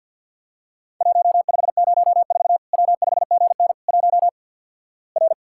Morse code, heard as a keyed tone, about 50 wpm, 700 Hz. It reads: 1504 W5GN 1 R
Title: Worked All Europe DX Contest - QTC Exchange - 50wpm